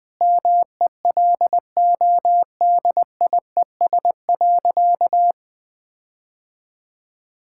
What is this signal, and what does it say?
Morse code, heard as a keyed tone, about 20 wpm, 700 Hz. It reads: MELODIES.
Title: All-Age Appropriate Jokes - 20wpm